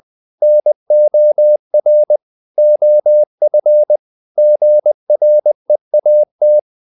Morse code, heard as a keyed tone, about 20 wpm, 600 Hz. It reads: NOR OF GREAT
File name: the old man and the sea / the old man and the sea part 1 20wpm